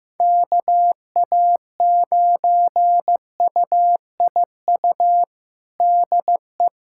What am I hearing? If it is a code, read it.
Morse code, heard as a keyed tone, about 15 wpm, 700 Hz. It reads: KA9UIU DE
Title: USA Callsign DE Callsign - 6 Characters - 15wpm